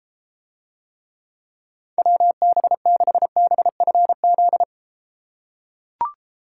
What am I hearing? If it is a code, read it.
Morse code, heard as a keyed tone, about 31 wpm, 700 Hz. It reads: WB6BFZ